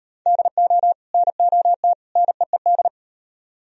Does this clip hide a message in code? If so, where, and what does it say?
Yes: Morse code from 0.3 to 2.9 s, heard as a keyed tone, about 35 wpm, 700 Hz. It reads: DO NOT NEED